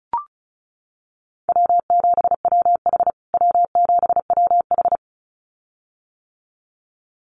Morse code, heard as a keyed tone, about 34 wpm, 700 Hz. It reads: W7WH W7WH
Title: USA Call Signs Repeated 2x 35wpm